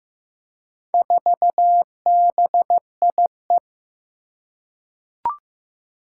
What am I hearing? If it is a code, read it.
Morse code, heard as a keyed tone, about 15 wpm, 700 Hz. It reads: 4BIE
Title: Four Challenging Letter-Number Combinations 15wpm